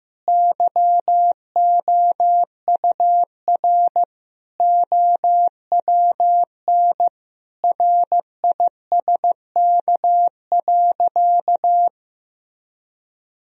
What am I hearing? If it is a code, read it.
Morse code, heard as a keyed tone, about 15 wpm, 700 Hz. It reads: YOUR OWN RISK.
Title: Sentences from Top 1000 Words - Review Part 4 - 15wpm